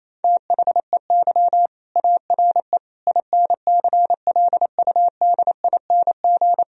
Morse code, heard as a keyed tone, about 28 wpm, 700 Hz. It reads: THEY ARE INCLUDING
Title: Sentences from Top 1000 Words - Review Part 4 - 28wpm